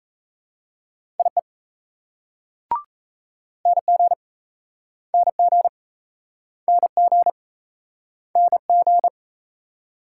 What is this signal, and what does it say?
Morse code, heard as a keyed tone, about 35 wpm, 700 Hz. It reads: IE NG NG NG NG
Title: Speed-Racing Multi-letter Phonemes - 28wpm